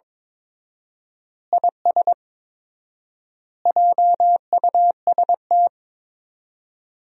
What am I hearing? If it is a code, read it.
Morse code, heard as a keyed tone, about 22 wpm, 700 Hz. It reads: IS JUST
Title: 4x Word-Spacing - Sentences from Top 200 Words - 22wpm